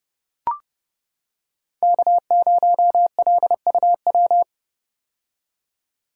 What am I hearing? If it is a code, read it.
Morse code, heard as a keyed tone, about 30 wpm, 700 Hz. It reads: K0LUW
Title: K1USN SST QSO Practice - 30wpm